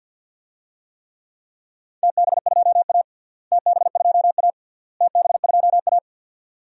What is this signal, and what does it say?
Morse code, heard as a keyed tone, about 47 wpm, 700 Hz. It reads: TB2A TB2A TB2A
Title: DXCC-100 Call Signs - 4 Characters - Repeated 3x - 50wpm